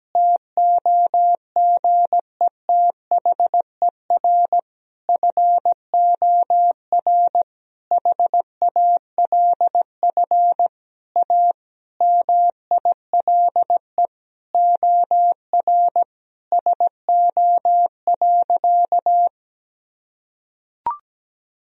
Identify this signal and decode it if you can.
Morse code, heard as a keyed tone, about 17 wpm, 700 Hz. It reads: TOGETHER FOR HALF A MILE OR SO.
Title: Sentences from Top 1000 Words - Review Part 1 - 17wpm